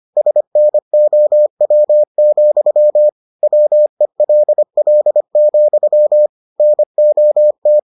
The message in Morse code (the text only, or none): SNOW, WELL, NOT